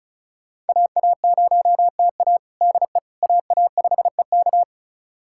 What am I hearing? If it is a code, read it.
Morse code, heard as a keyed tone, about 36 wpm, 700 Hz. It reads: AA0TA DE AA5EK
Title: USA Callsign DE Callsign - 5 Characters - 35wpm